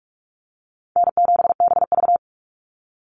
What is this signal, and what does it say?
Morse code, heard as a keyed tone, about 44 wpm, 700 Hz. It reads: N7BV
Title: Call Signs - 45wpm